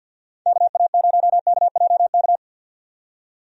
Morse code, heard as a keyed tone, about 51 wpm, 700 Hz. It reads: KA0KJK
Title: USA Call Signs - 6 Characters - 50wpm